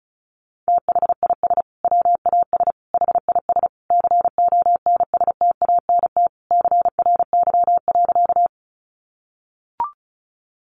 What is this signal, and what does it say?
Morse code, heard as a keyed tone, about 34 wpm, 700 Hz. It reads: THIS WAS HIS CONSTANT CRY.
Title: Sentences from Top 1500 Words 35wpm